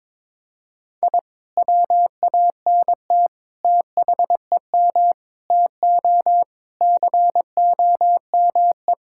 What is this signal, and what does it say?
Morse code, heard as a keyed tone, about 22 wpm, 700 Hz. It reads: I WANT THEM TO COME
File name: practice / 22wpm / Sentences with 2-Word Phrases - 22wpm